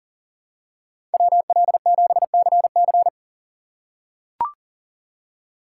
Morse code, heard as a keyed tone, about 40 wpm, 700 Hz. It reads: WL7CC
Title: USA Call Signs - 5 Characters - 2x Character Spacing - 40wpm